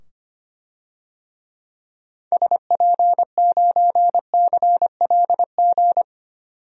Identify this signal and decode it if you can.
Morse code, heard as a keyed tone, about 25 wpm, 700 Hz. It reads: SP9CLG
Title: DXCC-100 Call Signs - 6 Characters - 25wpm